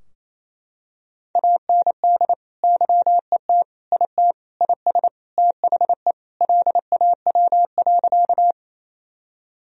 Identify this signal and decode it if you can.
Morse code, heard as a keyed tone, about 28 wpm, 700 Hz. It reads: AND YET IT IS THE LAW.
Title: Sentences from Top 1000 Words - Review Part 3 - 28wpm